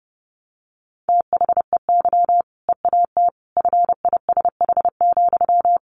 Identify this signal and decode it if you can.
Morse code, heard as a keyed tone, about 30 wpm, 700 Hz. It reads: THEY EAT FISH,
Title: Easy Phrases 30wpm